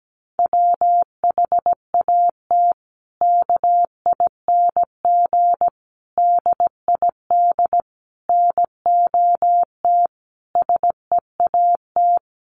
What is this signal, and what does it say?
Morse code, heard as a keyed tone, about 17 wpm, 700 Hz. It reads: WHAT KING DID NOT SEAT
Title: Sentences from Top 1500 Words 17wpm